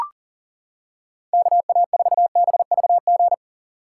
Morse code, heard as a keyed tone, about 40 wpm, 700 Hz. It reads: KA4BVG
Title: K1USN SST QSO Practice - 40wpm